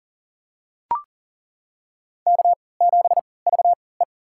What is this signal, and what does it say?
Morse code, heard as a keyed tone, about 40 wpm, spaced as for 23 wpm, 700 Hz. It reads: K7VE
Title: USA Call Signs - 3x Character Spacing - 40wpm